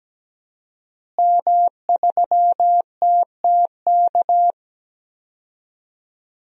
Morse code, heard as a keyed tone, about 17 wpm, 700 Hz. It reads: M3TTK
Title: DXCC-100 Call Signs - 5 Characters - Repeated 2x - 17wpm